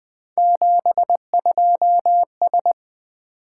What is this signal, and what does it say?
Morse code, heard as a keyed tone, about 20 wpm, 700 Hz. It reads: 72S
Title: Three Letter-Number Combinations - 20wpm